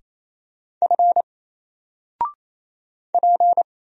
Morse code, heard as a keyed tone, about 27 wpm, 700 Hz. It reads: F P
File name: Single-Letter-Number-28wpm